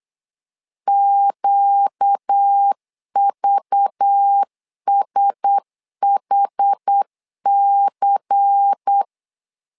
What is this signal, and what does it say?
Morse code, heard as a keyed tone, about 8 wpm, 800 Hz. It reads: QVSHC
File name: LEZIONI / 5 LBVC / 5 LBVC 8wpm 2935